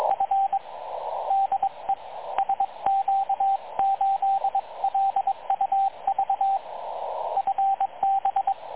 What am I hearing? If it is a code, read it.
Morse code, heard as a keyed tone, about 22 wpm, 780 Hz, with radio noise. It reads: F DE SQ8LUV FB
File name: dlazaawansowanych / nagraniazpasma / websdr recording 3567.0kHz